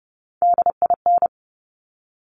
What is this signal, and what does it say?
Morse code, heard as a keyed tone, about 30 wpm, 700 Hz. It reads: DIN
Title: Top Three Letter Combinations in Words 30wpm